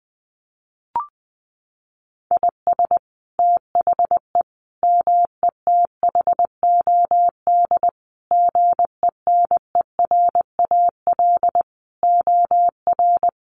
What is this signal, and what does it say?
Morse code, heard as a keyed tone, about 19 wpm, 700 Hz. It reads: IS THE METHOD GENERAL OR
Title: Sentences from Top 1500 Words 20wpm